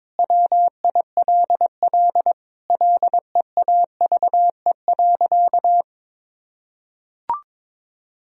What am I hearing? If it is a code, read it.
Morse code, heard as a keyed tone, about 22 wpm, 700 Hz. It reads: WILL LEAVE.
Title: Sentences from Top 1000 Words - Review Part 2 - 22wpm